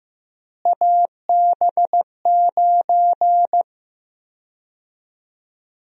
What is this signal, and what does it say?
Morse code, heard as a keyed tone, about 15 wpm, 700 Hz. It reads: AB9
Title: USA Call Sign Prefixes - Repeated 2x - 15wpm